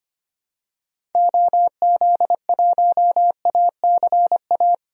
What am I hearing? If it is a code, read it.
Morse code, heard as a keyed tone, about 25 wpm, 700 Hz. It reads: OZ1ACA
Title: DXCC-100 Call Signs - 6 Characters - Repeated 3x - 25wpm